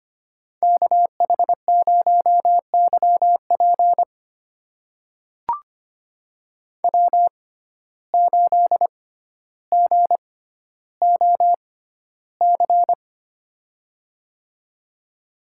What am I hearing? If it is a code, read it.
Morse code, heard as a keyed tone, about 25 wpm, spaced as for 13 wpm, 700 Hz. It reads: KH0YP W8GOC
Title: USA Call Signs - 5 Characters - 6x Character Spacing - 25wpm